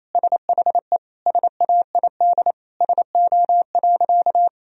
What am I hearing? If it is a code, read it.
Morse code, heard as a keyed tone, about 28 wpm, 700 Hz. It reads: SHE SAID SO.
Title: Sentences from Top 1000 Words - Review Part 4 - 28wpm